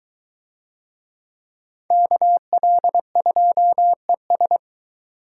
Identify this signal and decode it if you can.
Morse code, heard as a keyed tone, about 23 wpm, 700 Hz. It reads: KL2ES